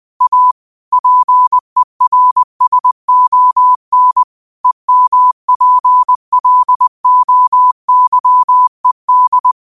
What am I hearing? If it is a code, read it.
Morse code, heard as a keyed tone, about 20 wpm, 1000 Hz. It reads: A PERSON EMPLOYED
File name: qotdfiles / 20wpm